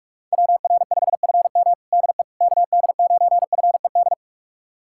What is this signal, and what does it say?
Morse code, heard as a keyed tone, about 46 wpm, 700 Hz. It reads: WR5FM DE KD9FED
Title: USA Callsign DE Callsign 45wpm